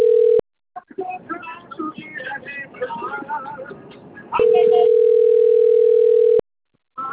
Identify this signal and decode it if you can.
Ringback tone (North America), 440 + 480 Hz.
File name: rg-1600-unknown- - - .60843